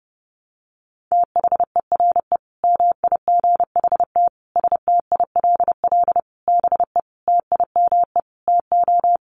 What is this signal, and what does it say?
Morse code, heard as a keyed tone, about 30 wpm, 700 Hz. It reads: THERE MIGHT STILL BE TIME TO
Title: Sentences from Top 1100 Words 30wpm